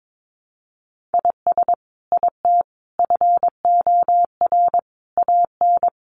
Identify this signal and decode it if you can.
Morse code, heard as a keyed tone, about 22 wpm, 700 Hz. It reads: IS IT FOR AN
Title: Sentences from Top 1200 Words 22wpm